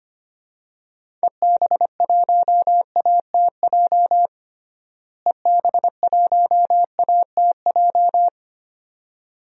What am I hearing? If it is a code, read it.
Morse code, heard as a keyed tone, about 25 wpm, 700 Hz. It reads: EB1ATJ EB1ATJ
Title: DXCC-100 Call Signs - 6 Characters - Repeated 2x - 25wpm